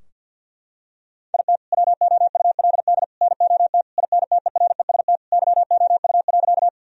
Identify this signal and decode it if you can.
Morse code, heard as a keyed tone, about 50 wpm, 700 Hz. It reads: IT WOULD NOT INTEREST YOU.
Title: Sentences from Top 1000 Words - Review Part 2 - 50wpm